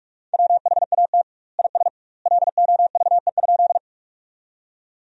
Morse code, heard as a keyed tone, about 46 wpm, 700 Hz. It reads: WHAT IS LOVE?
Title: 2x Word-Spacing - Sentences from Top 400 Words - 45wpm